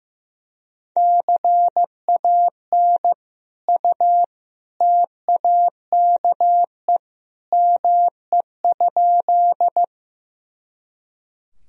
Morse code, heard as a keyed tone, about 15 wpm, 700 Hz. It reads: CAN U TAKE ME?
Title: Sentences from Top 200 Words and Abbreviations 15wpm